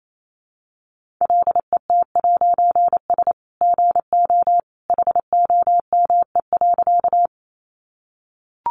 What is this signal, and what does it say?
Morse code, heard as a keyed tone, about 28 wpm, 700 Hz. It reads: LET'S GO HOME.
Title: A Sound Of Thunder 28wpm